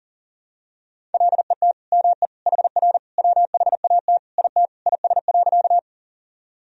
Morse code, heard as a keyed tone, about 40 wpm, 700 Hz. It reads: LET ME HR WHAT IT IS.
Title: Sentences from Top 300 Words and Abbreviations 40wpm